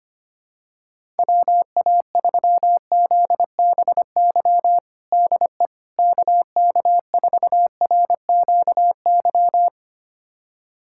Morse code, heard as a keyed tone, about 25 wpm, 700 Hz. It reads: WA3ZBY DE KK4RQY
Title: USA Callsign DE Callsign 25wpm